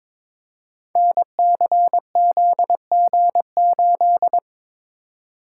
Morse code, heard as a keyed tone, about 23 wpm, 700 Hz. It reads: NCZG8
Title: Code Groups - 5 Letter:Numbers - 22wpm